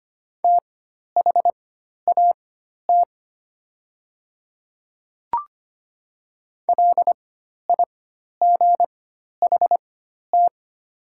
Morse code, heard as a keyed tone, about 25 wpm, spaced as for 12 wpm, 700 Hz. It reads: THAT LIGHT
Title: 4x Character Spacing - Top 300 Words - 25wpm